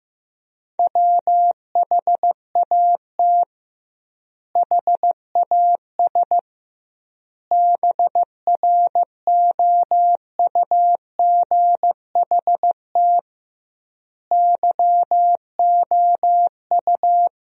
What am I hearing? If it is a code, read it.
Morse code, heard as a keyed tone, about 15 wpm, 700 Hz. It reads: WHAT HAS BROUGHT YOU
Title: 2x Word-Spacing - Sentences from Top 500 Words - 15wpm